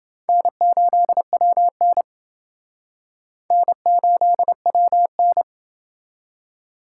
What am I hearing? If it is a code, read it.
Morse code, heard as a keyed tone, about 29 wpm, 700 Hz. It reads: N8WN N8WN